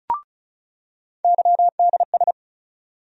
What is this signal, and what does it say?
Morse code, heard as a keyed tone, about 36 wpm, 700 Hz. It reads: YDS
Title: USA Call Sign Postfixes - 3 Letters 35wpm